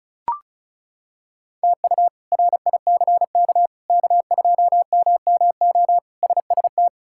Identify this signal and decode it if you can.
Morse code, heard as a keyed tone, about 34 wpm, 700 Hz. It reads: TU RICK K2MMO SST
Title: K1USN SST QSO Practice - 35wpm